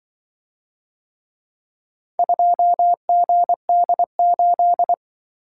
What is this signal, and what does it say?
Morse code, heard as a keyed tone, about 24 wpm, 700 Hz. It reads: 2GD8
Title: Speed-Racing - Four Letter-Number Combinations - 20wpm